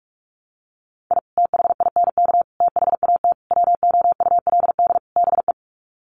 Morse code, heard as a keyed tone, about 44 wpm, 700 Hz. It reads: I THINK THAT WOULD BE
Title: Sets of 5 Words Encore2 45wpm